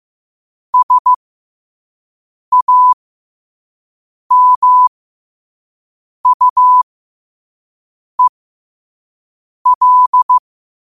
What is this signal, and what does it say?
Morse code, heard as a keyed tone, about 15 wpm, spaced as for 5 wpm, 1000 Hz. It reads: SAMUEL
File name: qotdfiles / 5wpm